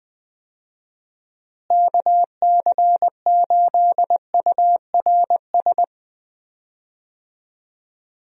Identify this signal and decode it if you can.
Morse code, heard as a keyed tone, about 19 wpm, 700 Hz. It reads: KC8URS
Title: USA Call Signs - 6 Characters - Repeated 3x - 20wpm